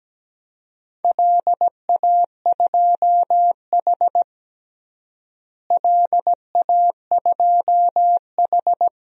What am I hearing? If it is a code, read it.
Morse code, heard as a keyed tone, about 17 wpm, 700 Hz. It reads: LA2H LA2H
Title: DXCC-100 Call Signs - 4 Characters - Repeated 2x - 17wpm